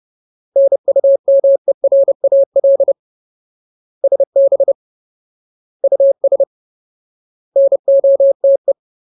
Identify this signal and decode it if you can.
Morse code, heard as a keyed tone, about 30 wpm, 550 Hz. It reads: NUMERAL SB US NOTE
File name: Ultimate Word List / 30WPM Ultimate